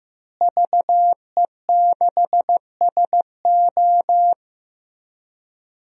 Morse code, heard as a keyed tone, about 15 wpm, 700 Hz. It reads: VE6SO
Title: DXCC-100 Call Signs - 5 Characters - Repeated 3x - 15wpm